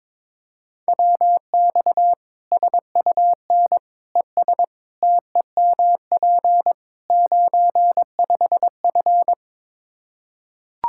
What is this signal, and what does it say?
Morse code, heard as a keyed tone, about 22 wpm, 700 Hz. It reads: WX SUN ES TEMP 95F
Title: K1USN SST QSO Practice 22wpm